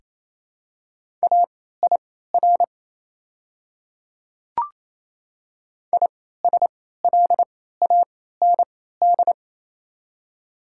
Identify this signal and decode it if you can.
Morse code, heard as a keyed tone, about 28 wpm, spaced as for 16 wpm, 700 Hz. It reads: AIR ISLAND
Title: 3x Character Spacing - Top 500 Words - 28wpm